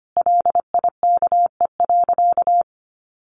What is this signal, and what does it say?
Morse code, heard as a keyed tone, about 25 wpm, 700 Hz. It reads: LIKE.
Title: Sentences from Top 1000 Words 25wpm